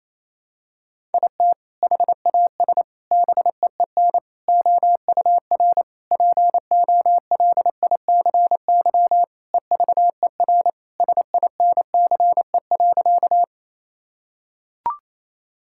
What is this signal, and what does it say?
Morse code, heard as a keyed tone, about 27 wpm, 700 Hz. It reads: IT HAS BEEN OUR POLICY EVER SINCE.
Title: Sentences from Top 1000 Words - Review Part 5 - 28wpm